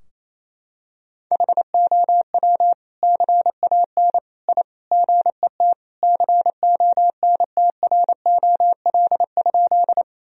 Morse code, heard as a keyed tone, about 28 wpm, 700 Hz. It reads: HOW CAN I GET CONTROL?
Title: Sentences from Top 1000 Words - Review Part 3 - 28wpm